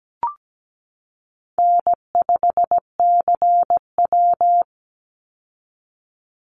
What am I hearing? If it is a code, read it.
Morse code, heard as a keyed tone, about 17 wpm, 700 Hz. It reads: N5CW
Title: Easy Call Signs Repeated 3x - 17wpm